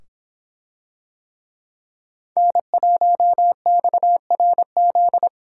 Morse code, heard as a keyed tone, about 26 wpm, 700 Hz. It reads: N1XRZ